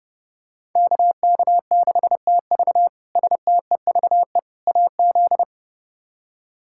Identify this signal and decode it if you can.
Morse code, heard as a keyed tone, about 29 wpm, 700 Hz. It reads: KK6TV STEVE AZ
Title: K1USN SST QSO Practice - 30wpm